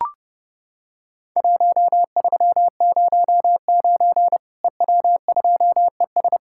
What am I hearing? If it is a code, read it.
Morse code, heard as a keyed tone, about 32 wpm, 700 Hz. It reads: 1309 EW2ES
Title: Worked All Europe DX Contest - QTC Exchange - 30wpm